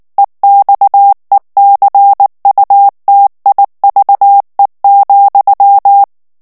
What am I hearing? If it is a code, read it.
Morse code, heard as a keyed tone, about 19 wpm, 800 Hz. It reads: EXECUTIVE,
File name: morse mp3 / ANGLAIS texte clair 3 vitesse 18 mots minute